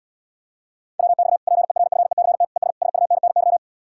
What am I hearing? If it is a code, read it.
Morse code, heard as a keyed tone, about 77 wpm, 700 Hz. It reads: WX PERFECT ES SUNNY